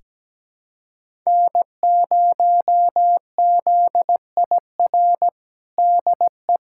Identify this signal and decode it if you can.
Morse code, heard as a keyed tone, about 17 wpm, 700 Hz. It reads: N0ZIR DE
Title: USA Callsign DE Callsign - 5 Characters - 17wpm